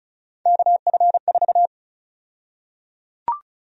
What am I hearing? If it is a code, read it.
Morse code, heard as a keyed tone, about 36 wpm, 700 Hz. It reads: KF4
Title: USA Call Sign Prefixes - Repeated 2x - 35wpm